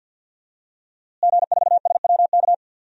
Morse code, heard as a keyed tone, about 48 wpm, 700 Hz. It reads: G4SWK
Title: DXCC-100 Call Signs - 5 Characters - Repeated 3x - 50wpm